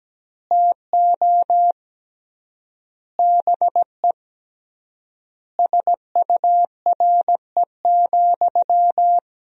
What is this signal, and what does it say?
Morse code, heard as a keyed tone, about 17 wpm, 700 Hz. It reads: TO BE SURE,
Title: 3x Word-Spacing - Sentences from Top 500 Words - 17wpm